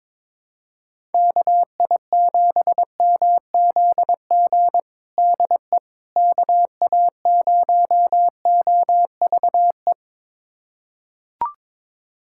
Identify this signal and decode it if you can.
Morse code, heard as a keyed tone, about 22 wpm, 700 Hz. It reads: KI7MZG DE KA0OVE
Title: USA Callsign DE Callsign 22wpm